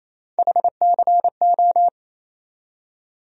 Morse code, heard as a keyed tone, about 28 wpm, 700 Hz. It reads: HCO
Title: USA Call Sign Postfixes - 3 Letters 28wpm